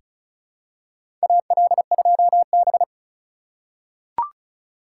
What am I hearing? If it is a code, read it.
Morse code, heard as a keyed tone, about 36 wpm, 700 Hz. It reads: AL2B
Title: USA Call Signs - 4 Characters - 4x Character Spacing - 35wpm